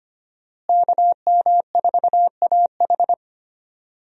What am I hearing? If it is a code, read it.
Morse code, heard as a keyed tone, about 25 wpm, 700 Hz. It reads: KM4AH
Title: USA Call Signs - 5 Characters - Repeated 2x - 25wpm